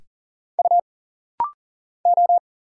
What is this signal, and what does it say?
Morse code, heard as a keyed tone, about 40 wpm, 700 Hz. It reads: U O
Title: Single Letters - Rapid-Fire 40wpm